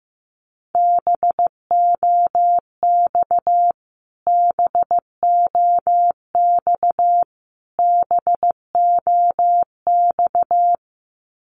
Morse code, heard as a keyed tone, about 15 wpm, 700 Hz. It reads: BOX BOX BOX
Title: Top 500 Words Repeated 15wpm